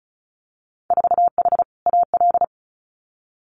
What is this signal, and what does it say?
Morse code, heard as a keyed tone, about 36 wpm, 700 Hz. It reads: 4H AL
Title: Winter Field Day Exchanges Encore 35wpm